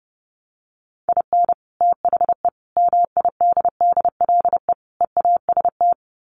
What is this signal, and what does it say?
Morse code, heard as a keyed tone, about 30 wpm, 700 Hz. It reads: IN THE MIDDLE EAST
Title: Sets of 4 Words 30wpm